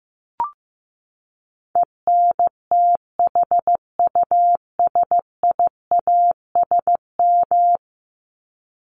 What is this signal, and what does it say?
Morse code, heard as a keyed tone, about 15 wpm, 700 Hz. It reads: ENTHUSIASM
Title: Top 1751-2000 Words 15wpm